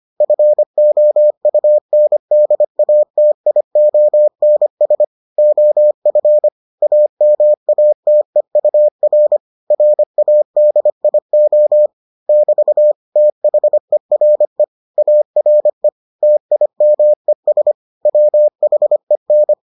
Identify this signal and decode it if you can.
Morse code, heard as a keyed tone, about 25 wpm, 600 Hz. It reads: FOUNDATIONS OF AMATEUR RADIO <BT> THERE ARE TIMES WHEN